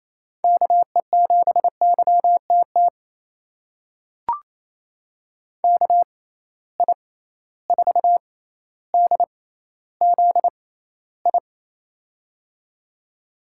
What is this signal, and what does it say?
Morse code, heard as a keyed tone, about 27 wpm, spaced as for 14 wpm, 700 Hz. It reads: KE7YTT KI4DZI
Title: USA Call Signs - 6 Characters - 6x Character Spacing - 28wpm